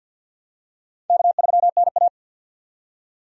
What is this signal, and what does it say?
Morse code, heard as a keyed tone, about 51 wpm, 700 Hz. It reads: K3NA
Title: USA Call Signs - 4 Characters - 50wpm